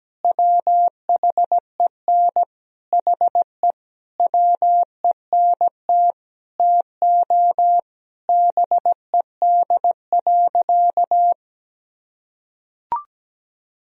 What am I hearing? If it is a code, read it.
Morse code, heard as a keyed tone, about 17 wpm, 700 Hz. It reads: WHEN HE WENT TO BED.
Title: Sentences from Top 1000 Words - Review Part 3 - 17wpm